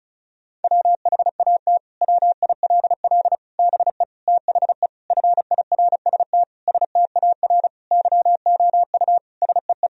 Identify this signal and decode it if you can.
Morse code, heard as a keyed tone, about 36 wpm, 700 Hz. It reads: WHAT WILL BE THE FIRST STAR YOU SEE
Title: Sentences from Top 1000 Words - Review Part 2 - 35wpm